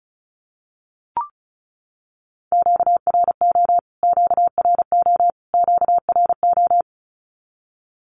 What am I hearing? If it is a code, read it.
Morse code, heard as a keyed tone, about 34 wpm, 700 Hz. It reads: QRO QRO QRO
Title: Q-Codes Repeated 3x 35wpm